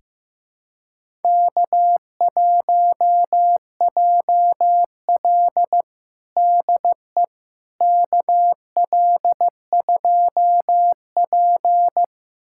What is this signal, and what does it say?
Morse code, heard as a keyed tone, about 15 wpm, 700 Hz. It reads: K1JL DE KL2P
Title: USA Callsign DE Callsign - 4 Characters - 15wpm